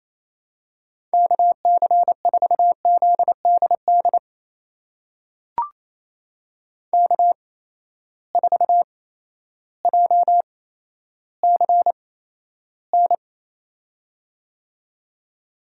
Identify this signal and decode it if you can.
Morse code, heard as a keyed tone, about 28 wpm, spaced as for 13 wpm, 700 Hz. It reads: KC4ZDD K4JCN
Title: USA Call Signs - 8x Character Spacing - 28wpm